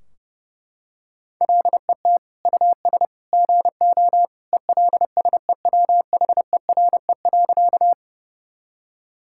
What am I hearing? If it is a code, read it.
Morse code, heard as a keyed tone, about 32 wpm, 700 Hz. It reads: LET US GO ELSEWHERE.
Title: Sentences from Top 1600 Words 30wpm